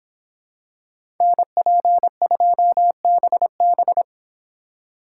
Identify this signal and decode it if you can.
Morse code, heard as a keyed tone, about 26 wpm, 700 Hz. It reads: NP2BB